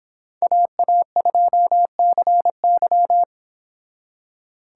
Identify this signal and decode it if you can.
Morse code, heard as a keyed tone, about 26 wpm, 700 Hz. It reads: AA2CY